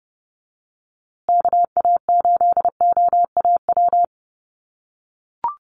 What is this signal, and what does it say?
Morse code, heard as a keyed tone, about 30 wpm, 700 Hz. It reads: KA8OAW
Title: USA Call Signs Repeated 3x 30wpm